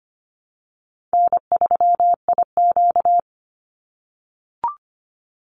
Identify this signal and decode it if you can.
Morse code, heard as a keyed tone, about 25 wpm, 700 Hz. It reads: N3IQ
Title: Easy Call Signs - 25wpm